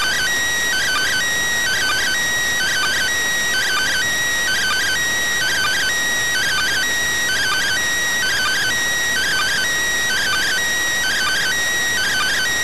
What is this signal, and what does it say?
Two-tone teleprinter signal, about 400 Hz shift.